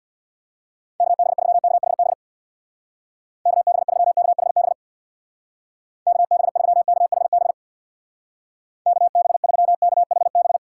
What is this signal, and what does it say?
Morse code, heard as a keyed tone, about 65 wpm, 700 Hz. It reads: KB3KHB KB3KHB KB3KHB KB3KHB